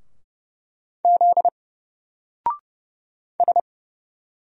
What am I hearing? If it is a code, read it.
Morse code, heard as a keyed tone, about 30 wpm, 700 Hz. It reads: Z S
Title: Single Character (Letter:Number:Punctuation:Prosign) Alt 30wpm